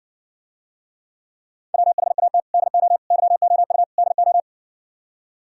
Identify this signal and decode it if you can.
Morse code, heard as a keyed tone, about 68 wpm, 700 Hz. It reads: WHAT DO YOU DO